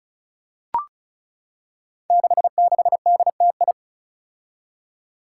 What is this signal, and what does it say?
Morse code, heard as a keyed tone, about 36 wpm, 700 Hz. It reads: 66DTI
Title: Five Challenging Letter-Number Combinations 35wpm